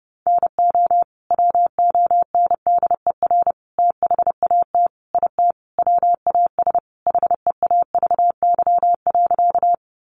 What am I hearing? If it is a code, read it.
Morse code, heard as a keyed tone, about 28 wpm, 700 Hz. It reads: NO WONDER THAT IT WAS HEAVY.
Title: Sentences from Top 1100 Words 30wpm